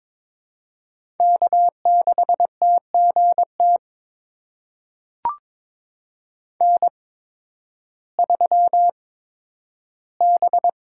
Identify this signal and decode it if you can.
Morse code, heard as a keyed tone, about 22 wpm, spaced as for 12 wpm, 700 Hz. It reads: K6TGT N3B
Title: USA Call Signs - 5 Characters - 8x Character Spacing - 22wpm